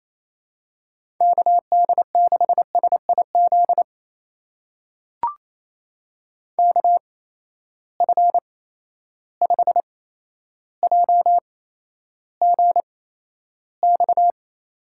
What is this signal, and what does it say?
Morse code, heard as a keyed tone, about 28 wpm, spaced as for 12 wpm, 700 Hz. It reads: KD6SIZ KF5JGX
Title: USA Call Signs - 6 Characters - 8x Character Spacing - 28wpm